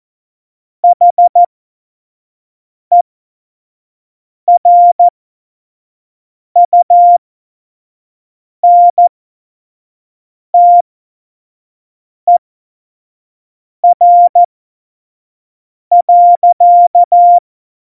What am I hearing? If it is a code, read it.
Morse code, heard as a keyed tone, about 14 wpm, spaced as for 5 wpm, 700 Hz. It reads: HERUNTER.
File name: Morsekurs / Witze / Witz 24. November 2020, 25 F70